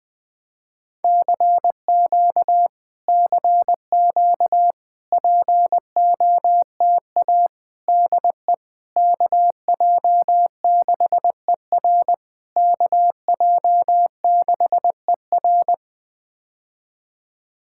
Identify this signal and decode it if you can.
Morse code, heard as a keyed tone, about 20 wpm, 700 Hz. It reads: CQ CQ POTA DE KJ6ER KJ6ER
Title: POTA QSO Practice - 20wpm